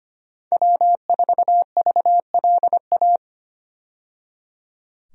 Morse code, heard as a keyed tone, about 25 wpm, 690 Hz. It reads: W4VLA
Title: USA Call Signs - 5 Characters - 25wpm